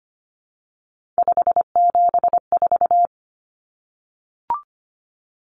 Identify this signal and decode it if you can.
Morse code, heard as a keyed tone, about 25 wpm, 700 Hz. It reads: 574
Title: Common Cut Numbers - 3-Digit 25wpm